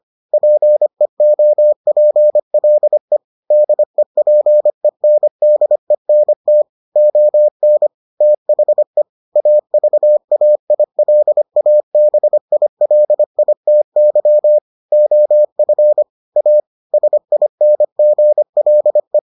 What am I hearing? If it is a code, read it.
Morse code, heard as a keyed tone, about 25 wpm, 600 Hz. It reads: PEOPLE DEPENDENT ON THE AVAILABILITY OF A SINGLE